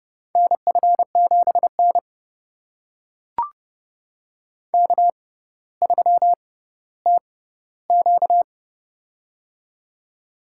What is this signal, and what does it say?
Morse code, heard as a keyed tone, about 30 wpm, spaced as for 15 wpm, 700 Hz. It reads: NF7N K3TQ
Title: USA Call Signs - 4 Characters - 6x Character Spacing - 30wpm